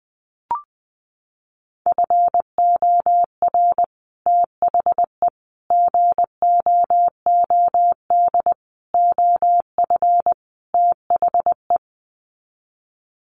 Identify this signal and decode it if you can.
Morse code, heard as a keyed tone, about 20 wpm, 700 Hz. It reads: FOR THE GOOD OF THE
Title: Sets of 5 Words Encore2 20wpm